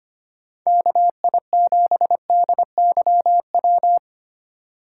Morse code, heard as a keyed tone, about 25 wpm, 700 Hz. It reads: KI7DYW
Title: USA Call Signs - 6 Characters - 2x Character Spacing - 25wpm